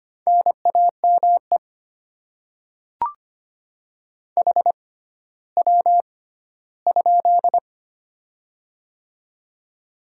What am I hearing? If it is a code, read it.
Morse code, heard as a keyed tone, about 25 wpm, spaced as for 14 wpm, 700 Hz. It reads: NAME HW?
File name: Top 20 QSO Elements - 6x Character Spacing - 25wpm